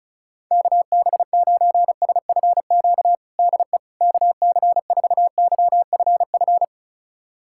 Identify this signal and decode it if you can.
Morse code, heard as a keyed tone, about 36 wpm, 700 Hz. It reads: KB9SFQ DE KC4YFF
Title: USA Callsign DE Callsign - 6 Characters - 35wpm